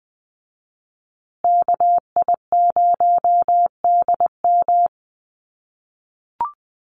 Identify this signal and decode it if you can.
Morse code, heard as a keyed tone, about 20 wpm, 700 Hz. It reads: KI0DM
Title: USA Call Signs Repeated 3x 20wpm